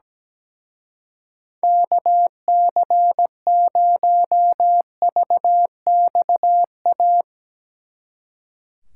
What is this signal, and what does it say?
Morse code, heard as a keyed tone, about 19 wpm, 700 Hz. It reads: KC0VXA
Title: Speed Racing - USA 2x3 Call Signs 17wpm